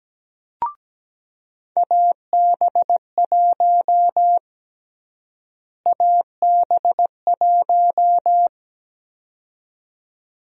Morse code, heard as a keyed tone, about 17 wpm, 700 Hz. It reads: AB1 AB1
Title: USA Call Sign Prefixes - Repeated 2x - 17wpm